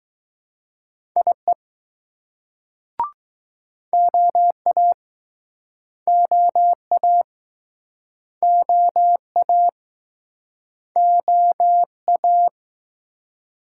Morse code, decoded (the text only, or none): IE OA OA OA OA